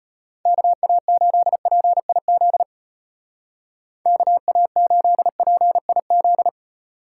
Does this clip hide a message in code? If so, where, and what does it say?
Yes: Morse code from 0.5 to 6.5 s, heard as a keyed tone, about 32 wpm, 700 Hz. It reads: KA8PIZ KA8PIZ